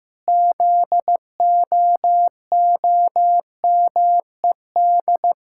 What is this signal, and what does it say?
Morse code, heard as a keyed tone, about 15 wpm, 700 Hz. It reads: ZOOMED
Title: Pangram Challenge v2 - 15wpm